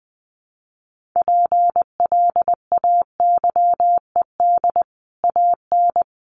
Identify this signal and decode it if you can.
Morse code, heard as a keyed tone, about 20 wpm, 700 Hz. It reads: PLAYED AN
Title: Sets of 5 Words Encore2 20wpm